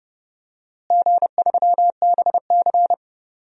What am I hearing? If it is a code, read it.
Morse code, heard as a keyed tone, about 31 wpm, 700 Hz. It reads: G3BC
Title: DXCC-100 Call Signs - 4 Characters - Repeated 2x - 30wpm